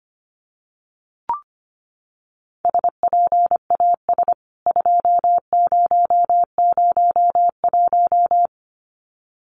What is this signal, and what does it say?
Morse code, heard as a keyed tone, about 25 wpm, 700 Hz. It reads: SPAS 2001
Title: CWT Exchange 25wpm